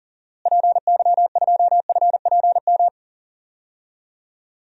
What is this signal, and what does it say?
Morse code, heard as a keyed tone, about 40 wpm, 700 Hz. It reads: PY2FPM
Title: DXCC-100 Call Signs - 6 Characters - Repeated 2x - 40wpm